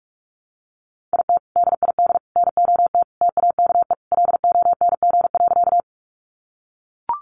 Morse code, heard as a keyed tone, about 44 wpm, 700 Hz. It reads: IT DID NOT TAKE LONG.
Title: Sentences from Top 600 Words 45wpm